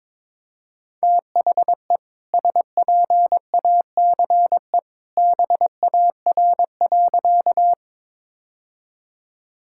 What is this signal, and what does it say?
Morse code, heard as a keyed tone, about 22 wpm, 700 Hz. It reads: THE SPACE BAR.
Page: morse-code-ninja-practice-22wpm